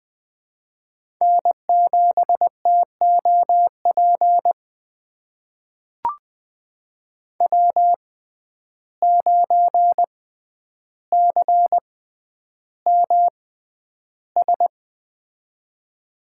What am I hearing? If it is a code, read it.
Morse code, heard as a keyed tone, about 20 wpm, spaced as for 10 wpm, 700 Hz. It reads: N7TOP W9CMS
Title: USA Call Signs - 5 Characters - 6x Character Spacing - 20wpm